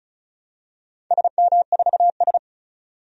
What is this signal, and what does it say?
Morse code, heard as a keyed tone, about 36 wpm, 700 Hz. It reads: SM4S
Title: DXCC-100 Call Signs - 4 Characters - 35wpm